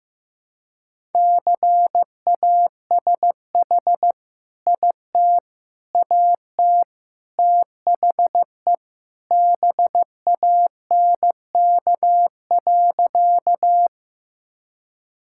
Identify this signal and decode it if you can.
Morse code, heard as a keyed tone, about 15 wpm, 700 Hz. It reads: CASH IT AT THE BANK.
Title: Sentences from Top 1000 Words - Review Part 5 - 15wpm